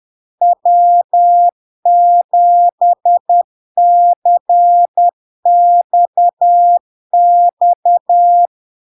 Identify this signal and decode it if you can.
Morse code, heard as a keyed tone, about 10 wpm, 700 Hz. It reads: W7CXX
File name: LARK-FD-2022-Log-in-CW-10wpm-700-Hz